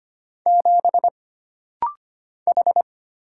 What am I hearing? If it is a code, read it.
Morse code, heard as a keyed tone, about 25 wpm, 700 Hz. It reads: 7 H
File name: Single Character (Letter:Number:Punctuation:Prosign) - Rapid-Fire - 25wpm